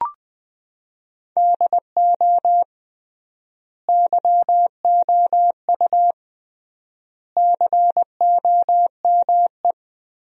Morse code, heard as a keyed tone, about 20 wpm, 700 Hz. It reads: DO YOU COME
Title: 3x Word-Spacing - Sentences from Top 200 Words - 20wpm